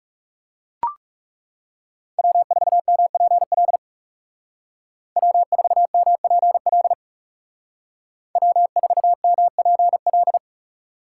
Morse code, heard as a keyed tone, about 39 wpm, 700 Hz. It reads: W4MPL W4MPL W4MPL